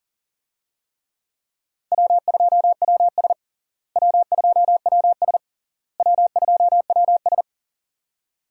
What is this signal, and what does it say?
Morse code, heard as a keyed tone, about 40 wpm, 700 Hz. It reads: W2WS W2WS W2WS
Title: USA Call Signs - 4 Characters - Repeated 3x - 40wpm